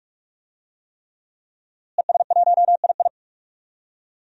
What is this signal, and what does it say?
Morse code, heard as a keyed tone, about 47 wpm, 700 Hz. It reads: ES1II
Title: DXCC-100 Call Signs - 5 Characters - Repeated 3x - 45wpm